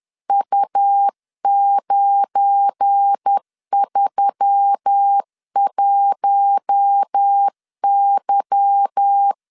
Morse code, heard as a keyed tone, about 10 wpm, 800 Hz. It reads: U931Y
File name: LEZIONI / 8 YJ45 / 8 Y 10wpm 2615